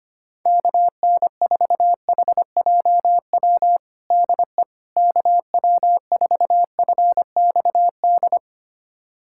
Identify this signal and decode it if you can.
Morse code, heard as a keyed tone, about 25 wpm, 700 Hz. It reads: KN4HJW DE KW4FXD
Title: USA Callsign DE Callsign - 6 Characters - 25wpm